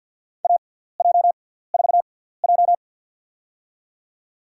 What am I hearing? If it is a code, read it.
Morse code, heard as a keyed tone, about 50 wpm, spaced as for 17 wpm, 690 Hz. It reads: AJ4J
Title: USA Call Signs - 4 Characters - 6x Character Spacing - 50wpm